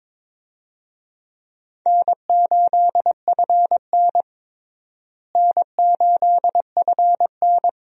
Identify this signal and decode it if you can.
Morse code, heard as a keyed tone, about 21 wpm, 700 Hz. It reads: N8FN N8FN
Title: USA Call Signs - 4 Characters - Repeated 2x - 22wpm